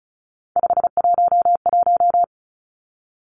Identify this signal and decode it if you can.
Morse code, heard as a keyed tone, about 36 wpm, 700 Hz. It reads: 511
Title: Three Challenging Letter-Number Combinations 35wpm